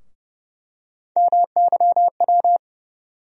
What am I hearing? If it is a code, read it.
Morse code, heard as a keyed tone, about 30 wpm, 700 Hz. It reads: MYW
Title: USA Call Sign Postfixes - 3 Letters 30wpm